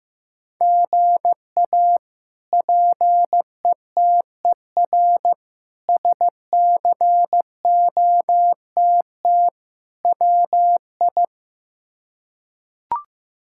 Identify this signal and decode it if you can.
Morse code, heard as a keyed tone, about 15 wpm, 700 Hz. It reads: GA PETER SCOTT WI
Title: K1USN SST QSO Practice - 15wpm